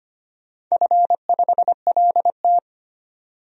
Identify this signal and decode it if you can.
Morse code, heard as a keyed tone, about 25 wpm, 690 Hz. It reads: F5LT
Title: DXCC-100 Call Signs - 4 Characters - 25wpm